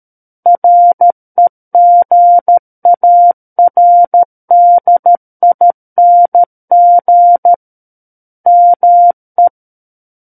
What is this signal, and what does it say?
Morse code, heard as a keyed tone, about 13 wpm, 700 Hz. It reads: REGARDING ME
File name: Book One / 04-Chapter 4 - 13 WPM